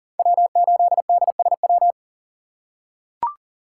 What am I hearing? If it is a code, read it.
Morse code, heard as a keyed tone, about 40 wpm, 700 Hz. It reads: W8DSW
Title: USA Call Signs - 4x Character Spacing - 40wpm